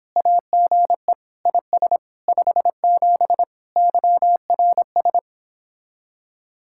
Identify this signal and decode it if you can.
Morse code, heard as a keyed tone, about 29 wpm, 700 Hz. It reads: AGE IS 57 YRS